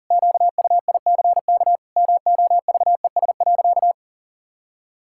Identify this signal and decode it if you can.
Morse code, heard as a keyed tone, about 40 wpm, 700 Hz. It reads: QUICK MOVES.
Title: Pangram Challenge v2 - 40wpm